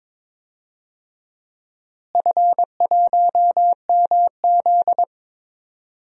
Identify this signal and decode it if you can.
Morse code, heard as a keyed tone, about 22 wpm, 700 Hz. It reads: F1MZ
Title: DXCC-100 Call Signs - 4 Characters - Repeated 3x - 22wpm